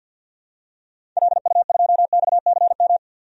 Morse code, heard as a keyed tone, about 51 wpm, 700 Hz. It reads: LU2XCM
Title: DXCC-100 Call Signs - 6 Characters - 50wpm